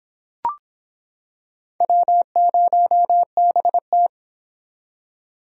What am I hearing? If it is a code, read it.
Morse code, heard as a keyed tone, about 26 wpm, 700 Hz. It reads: W0BT